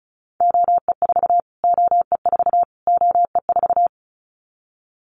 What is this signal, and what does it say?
Morse code, heard as a keyed tone, about 34 wpm, 700 Hz. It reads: OE4 OE4 OE4
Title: Callsign Prefixes for Top 25 DXCC - Repeated 3x - 35wpm